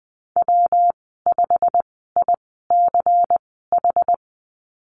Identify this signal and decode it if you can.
Morse code, heard as a keyed tone, about 20 wpm, spaced as for 15 wpm, 700 Hz. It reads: W5ICH
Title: 179 Lesson 29g - Slash - Call Signs 20wpm